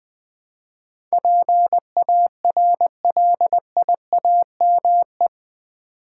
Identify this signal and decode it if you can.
Morse code, heard as a keyed tone, about 20 wpm, 700 Hz. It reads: PARLIAME
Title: Build A Word - 4 Letters At A Time - 10-Letter Words - 20wpm